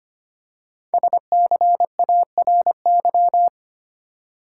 Morse code, heard as a keyed tone, about 25 wpm, 700 Hz. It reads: SCARY
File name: 5 Letter Words - First 3 Letters Given - 25wpm